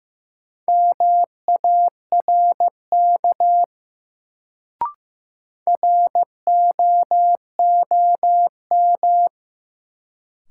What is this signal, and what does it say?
Morse code, heard as a keyed tone, about 15 wpm, 700 Hz. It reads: MARK ROOM
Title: Sentences from Top 300 Words - New Word Drill - 15wpm